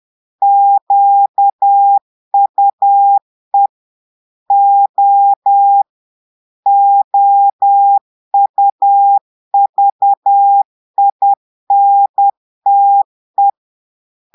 Morse code, heard as a keyed tone, about 10 wpm, 800 Hz. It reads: QUE O OUVINTE
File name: portugais texte en clair 2 vitesse 10 mots minutes